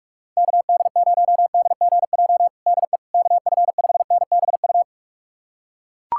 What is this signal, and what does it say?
Morse code, heard as a keyed tone, about 41 wpm, 700 Hz. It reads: KD0DGJ DE KF5NBU
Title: USA Callsign DE Callsign - 6 Characters - 45wpm